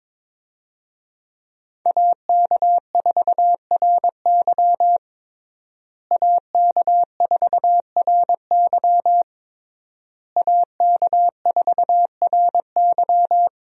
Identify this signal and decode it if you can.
Morse code, heard as a keyed tone, about 22 wpm, 700 Hz. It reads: AK4RY AK4RY AK4RY
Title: USA Call Signs - 5 Characters - Repeated 3x - 22wpm